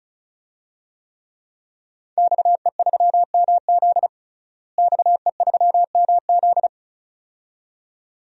Morse code, heard as a keyed tone, about 34 wpm, 700 Hz. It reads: XE3MZ XE3MZ
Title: DXCC-100 Call Signs - 5 Characters - Repeated 2x - 35wpm